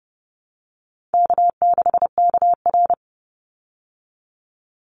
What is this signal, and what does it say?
Morse code, heard as a keyed tone, about 30 wpm, 700 Hz. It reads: K6KR
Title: Easy Call Signs Repeated 2x - 30wpm